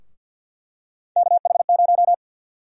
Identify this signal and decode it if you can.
Morse code, heard as a keyed tone, about 50 wpm, 700 Hz. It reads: KH0
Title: USA Call Sign Prefixes - 50wpm